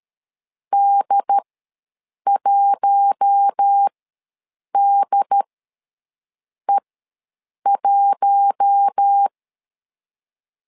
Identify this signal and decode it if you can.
Morse code, heard as a keyed tone, about 13 wpm, spaced as for 7 wpm, 800 Hz. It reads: D1DE1
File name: LEZIONI / 4 Q1 ws 12wpm 2129